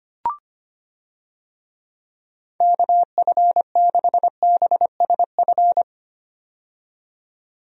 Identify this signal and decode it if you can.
Morse code, heard as a keyed tone, about 25 wpm, 700 Hz. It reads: KF6BSF
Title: USA Call Signs - 6 Characters - 25wpm